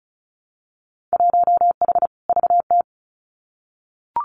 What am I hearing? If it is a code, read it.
Morse code, heard as a keyed tone, about 36 wpm, 700 Hz. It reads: 1H VT
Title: Winter Field Day Exchanges Encore 35wpm